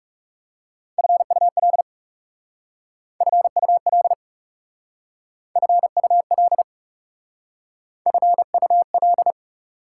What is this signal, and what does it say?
Morse code, heard as a keyed tone, about 38 wpm, 700 Hz. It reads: FUL FUL FUL FUL